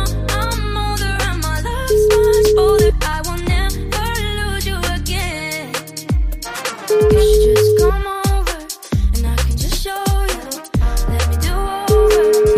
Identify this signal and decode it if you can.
Ringback tone (Europe, 425 Hz).